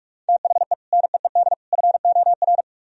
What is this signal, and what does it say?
Morse code, heard as a keyed tone, about 44 wpm, 700 Hz. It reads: THE NEED FOR